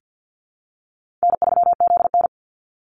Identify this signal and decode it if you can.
Morse code, heard as a keyed tone, about 50 wpm, 700 Hz. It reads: N3ZN
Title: Call Signs Repeated 2x - 50wpm